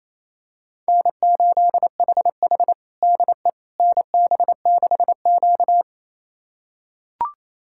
Morse code, heard as a keyed tone, about 28 wpm, 700 Hz. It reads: N8HH DE NB6Q
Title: USA Callsign DE Callsign - 4 Characters - 28wpm